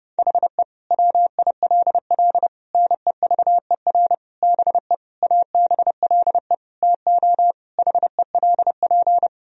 Morse code, heard as a keyed tone, about 32 wpm, 700 Hz. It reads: HE WILL NEVER BE ABLE TO HELP
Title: Sentences from Top 1000 Words - Review Part 2 - 30wpm